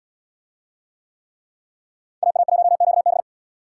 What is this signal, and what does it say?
Morse code, heard as a keyed tone, about 77 wpm, 700 Hz. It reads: AA1QB